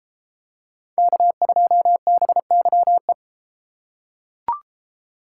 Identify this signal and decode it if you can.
Morse code, heard as a keyed tone, about 32 wpm, 700 Hz. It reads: K2BYE